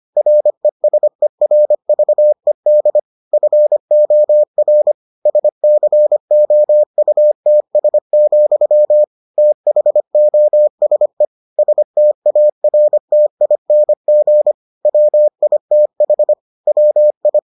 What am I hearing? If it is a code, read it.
Morse code, heard as a keyed tone, about 25 wpm, 600 Hz. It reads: RESERVED FOR SCOUTS, THOSE STARTING WITH WI